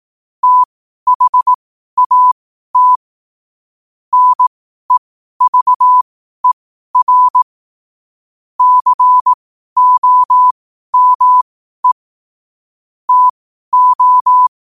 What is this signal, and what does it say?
Morse code, heard as a keyed tone, about 18 wpm, spaced as for 12 wpm, 1000 Hz. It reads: THAT NEVER COME TO
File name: qotdfiles / 12wpm